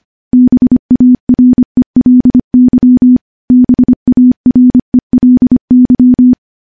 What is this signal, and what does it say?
Morse code, heard as a keyed tone, about 25 wpm, 250 Hz. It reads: BARELY BARELY
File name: morse-training-25WPM